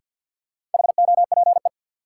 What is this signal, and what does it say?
Morse code, heard as a keyed tone, about 52 wpm, 700 Hz. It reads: HOPE
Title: Sentences from Top 1000 Words - New Word Drill - 50wpm